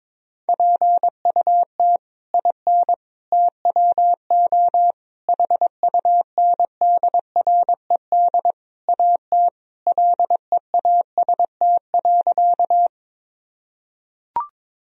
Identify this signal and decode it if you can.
Morse code, heard as a keyed tone, about 22 wpm, 700 Hz. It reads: PUT IN TWO HUNDRED AT LEAST.
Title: Sentences from Top 1000 Words - Review Part 2 - 22wpm